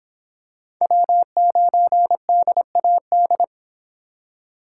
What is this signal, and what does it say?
Morse code, heard as a keyed tone, about 25 wpm, 700 Hz. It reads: W9DAD